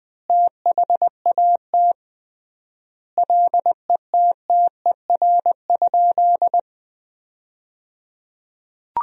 Morse code, heard as a keyed tone, about 20 wpm, 700 Hz. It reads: THAT LETTER?
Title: 3x Word-Spacing - Sentences from Top 300 Words - 20wpm